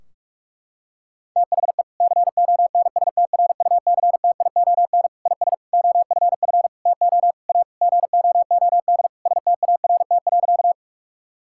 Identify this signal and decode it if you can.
Morse code, heard as a keyed tone, about 46 wpm, 700 Hz. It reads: THE CONSTRUCTION IS OFF TO A GOOD START.
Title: Sentences from Top 1600 Words 45wpm